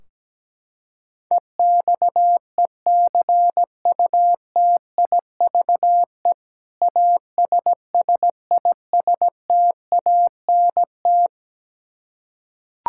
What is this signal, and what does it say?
Morse code, heard as a keyed tone, about 17 wpm, 700 Hz. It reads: EXECUTIVE ASSISTANT
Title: Practice QSO - Exchange 3 - More Info - 17wpm